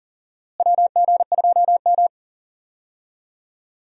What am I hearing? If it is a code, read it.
Morse code, heard as a keyed tone, about 40 wpm, 700 Hz. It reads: WG2M
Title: USA Call Signs - 4 Characters - Repeated 2x - 40wpm